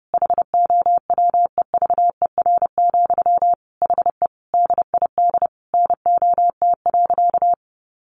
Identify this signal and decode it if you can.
Morse code, heard as a keyed tone, about 28 wpm, 700 Hz. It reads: HOWEVER, HE DID NOT.
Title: Sentences from Top 600 Words 30wpm